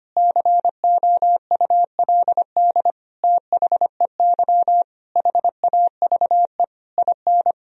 Morse code, heard as a keyed tone, about 25 wpm, 700 Hz. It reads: COULD THEY HAVE IN
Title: Sentences from Top 1000 Words - Review Part 3 - 25wpm